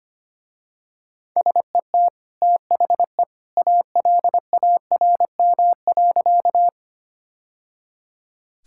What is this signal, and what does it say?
Morse code, heard as a keyed tone, about 25 wpm, 700 Hz. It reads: SET THE ALARM.
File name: Easy Sentences with Hints - 25wpm